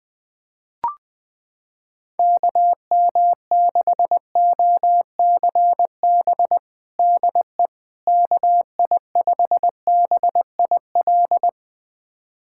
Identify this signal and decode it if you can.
Morse code, heard as a keyed tone, about 20 wpm, 700 Hz. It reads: KM6OCB DE KI5BIL
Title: USA Callsign DE Callsign - 6 Characters - 20wpm